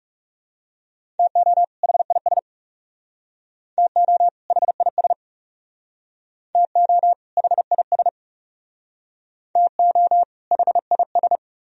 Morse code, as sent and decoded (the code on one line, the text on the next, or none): - --- / .... .. ... / - --- / .... .. ... / - --- / .... .. ... / - --- / .... .. ...
TO HIS TO HIS TO HIS TO HIS